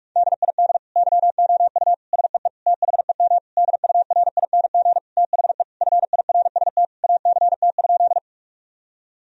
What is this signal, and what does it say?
Morse code, heard as a keyed tone, about 46 wpm, 700 Hz. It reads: DID YOU SEE THEM DURING THE FIRST ACT?
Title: Sentences from Top 1000 Words - Review Part 2 - 45wpm